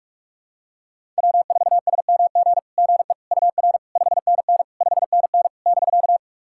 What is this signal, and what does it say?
Morse code, heard as a keyed tone, about 41 wpm, 690 Hz. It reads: W4SMG GE UR 5NN 5NN <BK>
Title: SOTA QSO Practice 45wpm